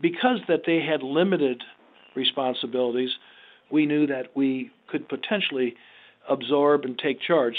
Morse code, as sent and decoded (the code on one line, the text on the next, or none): none